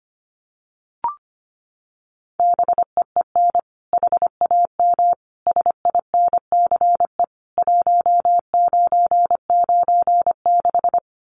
Morse code, heard as a keyed tone, about 25 wpm, 700 Hz. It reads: BEEN HAM SINCE 1996
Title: Common QSO Phrases 25wpm